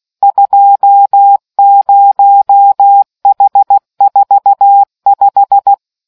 Morse code, heard as a keyed tone, about 16 wpm, 800 Hz. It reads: 20H45